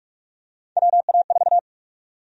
Morse code, heard as a keyed tone, about 44 wpm, 700 Hz. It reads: WA4
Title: USA Call Sign Prefixes - 45wpm